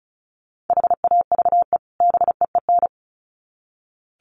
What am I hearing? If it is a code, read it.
Morse code, heard as a keyed tone, about 34 wpm, 700 Hz. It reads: HAVE BEEN
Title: Sets of 2 Words 35wpm